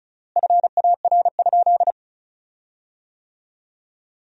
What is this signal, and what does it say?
Morse code, heard as a keyed tone, about 36 wpm, 700 Hz. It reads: FAR?
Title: 3x Word-Spacing - Sentences from Top 300 Words - 35wpm